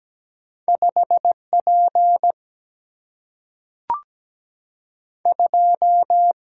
Morse code, heard as a keyed tone, about 17 wpm, 700 Hz. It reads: 5 P 2
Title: Single-Letter Single-Number 17wpm